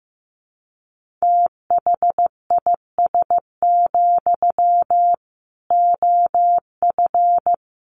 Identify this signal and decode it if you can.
Morse code, heard as a keyed tone, about 14 wpm, 700 Hz. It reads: THIS, OF
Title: Sentences from Top 1400 Words 15wpm